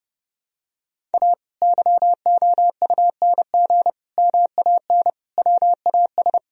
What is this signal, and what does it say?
Morse code, heard as a keyed tone, about 30 wpm, 700 Hz. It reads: A YOUNG MAN WAS
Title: Sentences from Top 1000 Words - Review Part 3 - 30wpm